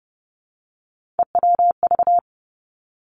Morse code, heard as a keyed tone, about 30 wpm, 700 Hz. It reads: EWV
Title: Three Challenging Letter-Number Combinations 30wpm